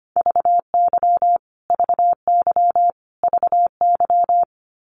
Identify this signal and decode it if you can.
Morse code, heard as a keyed tone, about 25 wpm, 700 Hz. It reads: VY VY VY
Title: Abbreviations Repeated 3x - 25wpm